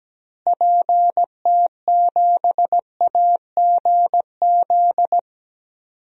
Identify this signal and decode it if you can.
Morse code, heard as a keyed tone, about 17 wpm, 700 Hz. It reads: PT7AGZ
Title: DXCC-100 Call Signs - 6 Characters - Repeated 3x - 17wpm